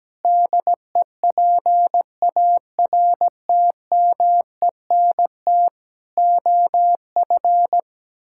Morse code, heard as a keyed tone, about 17 wpm, 700 Hz. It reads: DEPARTMENT OF
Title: Sentences from Top 1000 Words - Review Part 5 - 17wpm